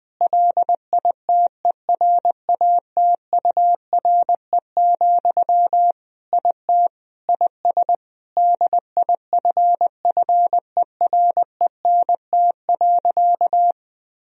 Morse code, heard as a keyed tone, about 20 wpm, 700 Hz. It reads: LITERATURE, IT IS DIFFERENT.
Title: Sentences from Top 1000 Words - Review Part 3 - 20wpm